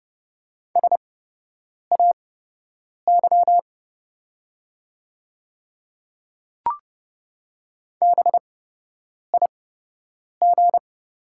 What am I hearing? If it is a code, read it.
Morse code, heard as a keyed tone, about 30 wpm, spaced as for 8 wpm, 700 Hz. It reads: SAY BIG
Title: 8x Character Spacing - Top 200 Words - 30wpm